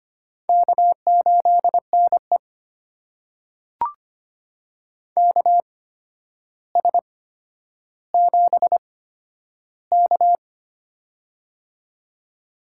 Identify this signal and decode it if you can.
Morse code, heard as a keyed tone, about 25 wpm, spaced as for 11 wpm, 700 Hz. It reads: K8NE KS7K
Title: USA Call Signs - 4 Characters - 8x Character Spacing - 25wpm